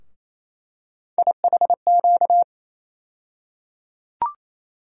Morse code, heard as a keyed tone, about 27 wpm, 700 Hz. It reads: IHQ
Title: USA Call Sign Postfixes - 3 Letters 28wpm